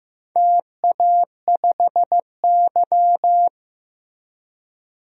Morse code, heard as a keyed tone, about 15 wpm, 700 Hz. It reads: TA5Y
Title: DXCC-100 Call Signs - 4 Characters - Repeated 3x - 15wpm